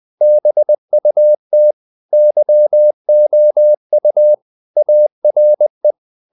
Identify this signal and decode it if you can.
Morse code, heard as a keyed tone, about 20 wpm, 600 Hz. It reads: BUT YOU ARE
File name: the old man and the sea / the old man and the sea part 1 20wpm